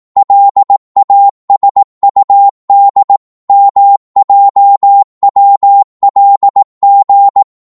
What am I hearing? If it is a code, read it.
Morse code, heard as a keyed tone, about 18 wpm, 800 Hz. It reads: LASUD MJWLG
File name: Morse / ltr-18wpm